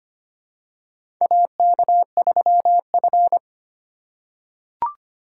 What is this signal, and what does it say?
Morse code, heard as a keyed tone, about 25 wpm, 700 Hz. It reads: AK3F
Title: USA Call Signs - 4 Characters - 3x Character Spacing - 25wpm